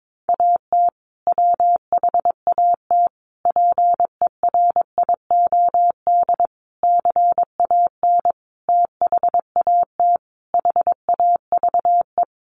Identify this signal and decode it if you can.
Morse code, heard as a keyed tone, about 22 wpm, 700 Hz. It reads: AT WHAT PERIOD CAN THAT HAVE
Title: Sentences from Top 800 Words 22wpm